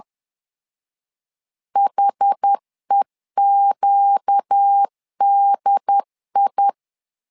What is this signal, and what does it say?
Morse code, heard as a keyed tone, about 10 wpm, 800 Hz. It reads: HEQDI
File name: LEZIONI / 4 Q 10wpm 2210